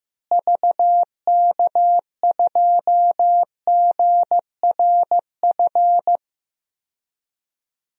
Morse code, heard as a keyed tone, about 15 wpm, 700 Hz. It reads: VK2GRF
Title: DXCC-100 Call Signs - 6 Characters - Repeated 3x - 15wpm